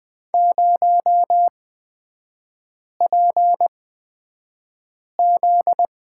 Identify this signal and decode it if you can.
Morse code, heard as a keyed tone, about 20 wpm, spaced as for 5 wpm, 700 Hz. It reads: 0PZ
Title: 243 Lesson 39e - Letter z - Three Characters 20wpm